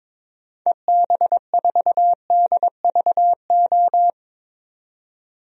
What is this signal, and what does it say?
Morse code, heard as a keyed tone, about 22 wpm, 700 Hz. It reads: EB4DVO
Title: DXCC-100 Call Signs - 6 Characters - 22wpm